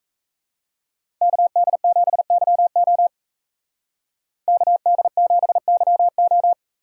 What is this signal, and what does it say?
Morse code, heard as a keyed tone, about 35 wpm, 700 Hz. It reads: KD7YO KD7YO